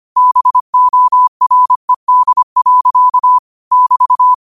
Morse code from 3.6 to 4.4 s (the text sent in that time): <BT>